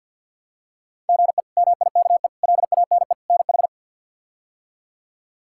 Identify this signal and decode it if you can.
Morse code, heard as a keyed tone, about 50 wpm, 700 Hz. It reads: GE MIKE LANE NH
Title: K1USN SST QSO Practice - 50wpm